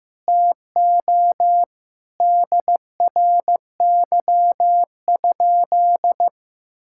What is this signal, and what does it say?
Morse code, heard as a keyed tone, about 15 wpm, 700 Hz. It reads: TO DRY?
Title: Sentences from Top 1600 Words 15wpm